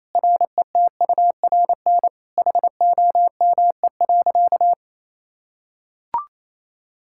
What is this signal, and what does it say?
Morse code, heard as a keyed tone, about 28 wpm, 700 Hz. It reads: RETURN HOME.
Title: Sentences from Top 1000 Words - Review Part 5 - 28wpm